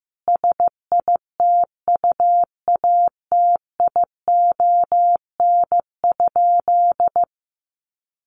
Morse code, heard as a keyed tone, about 15 wpm, 700 Hz. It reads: SITUATION?
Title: Sentences from Top 1200 Words 15wpm